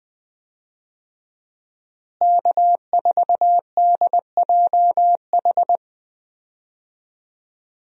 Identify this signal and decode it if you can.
Morse code, heard as a keyed tone, about 21 wpm, 700 Hz. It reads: K4DJH
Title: USA Call Signs - 5 Characters - Repeated 2x - 20wpm